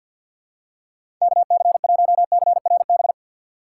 Morse code, heard as a keyed tone, about 53 wpm, 700 Hz. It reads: KC1CRB
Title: USA Call Signs - 6 Characters - 3x Character Spacing - 50wpm